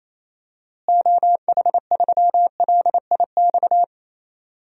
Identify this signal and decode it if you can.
Morse code, heard as a keyed tone, about 28 wpm, 700 Hz. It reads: OH3LIX
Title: DXCC-100 Call Signs - 6 Characters - Repeated 2x - 28wpm